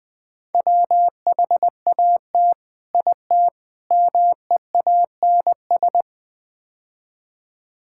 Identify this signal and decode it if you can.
Morse code, heard as a keyed tone, about 21 wpm, 700 Hz. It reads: WHAT IT MEANS
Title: Build A Phrase - 5-Word Phrases - 20wpm